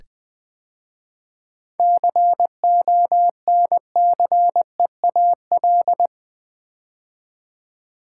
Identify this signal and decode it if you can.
Morse code, heard as a keyed tone, about 20 wpm, 700 Hz. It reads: CONCEAL
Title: 7 Letter Words - First 3 Letters Given - 20wpm